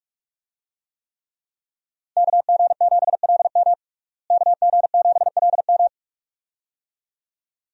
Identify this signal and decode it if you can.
Morse code, heard as a keyed tone, about 41 wpm, 700 Hz. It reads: KG7LM KG7LM
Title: USA Call Signs - 5 Characters - Repeated 2x - 45wpm